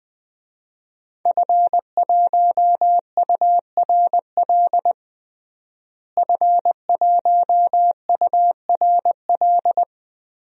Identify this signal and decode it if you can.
Morse code, heard as a keyed tone, about 21 wpm, 700 Hz. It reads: F1URL F1URL
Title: DXCC-100 Call Signs - 5 Characters - Repeated 3x - 20wpm